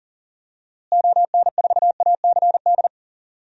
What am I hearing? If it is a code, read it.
Morse code, heard as a keyed tone, about 39 wpm, 700 Hz. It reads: ON4ACD
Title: DXCC-100 Call Signs - 6 Characters - Repeated 2x - 40wpm